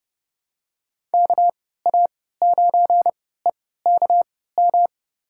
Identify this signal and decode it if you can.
Morse code, heard as a keyed tone, about 32 wpm, spaced as for 17 wpm, 700 Hz. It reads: KA9EKM
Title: USA Call Signs - 6 Characters - 3x Character Spacing - 30wpm